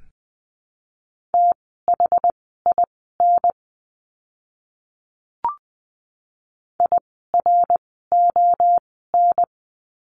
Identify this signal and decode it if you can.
Morse code, heard as a keyed tone, about 20 wpm, spaced as for 15 wpm, 700 Hz. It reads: THIN IRON
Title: 061 Lesson 11f - Letter h - Words 20wpm